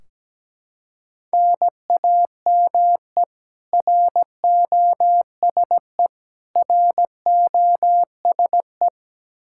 Morse code, heard as a keyed tone, about 17 wpm, 700 Hz. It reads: NAME ROSE ROSE
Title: Practice QSO - Exchange 1 - Beginning - 17wpm